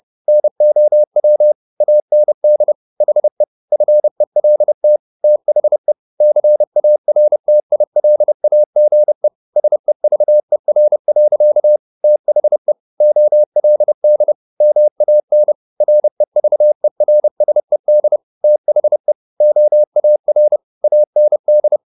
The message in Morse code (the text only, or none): NOW AND HE FELT THE CARTILAGE SEVER. THE OLD MAN REVERSED THE OAR AND